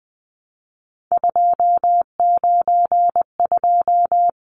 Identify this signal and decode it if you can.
Morse code, heard as a keyed tone, about 20 wpm, 700 Hz. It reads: 292
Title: Numbers - 3-Digit 20wpm